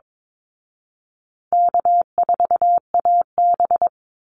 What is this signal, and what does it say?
Morse code, heard as a keyed tone, about 22 wpm, 700 Hz. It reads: K4AB
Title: Call Signs Repeated 2x - 22wpm